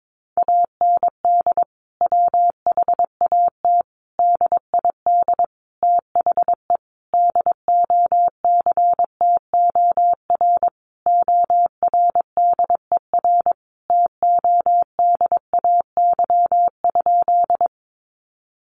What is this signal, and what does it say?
Morse code, heard as a keyed tone, about 22 wpm, 700 Hz. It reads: AND WHAT DID THE DOCTOR ORDER TODAY?
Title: Sentences from Top 1500 Words 22wpm